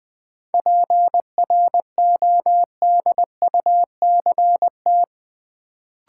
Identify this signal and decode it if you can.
Morse code, heard as a keyed tone, about 20 wpm, 700 Hz. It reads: PRODUCT
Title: Sentences from Top 1400 Words - New Word Drill Alt - 20wpm